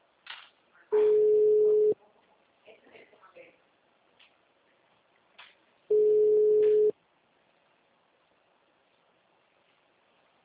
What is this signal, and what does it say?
Ringback tone (Europe, 425 Hz), with speech.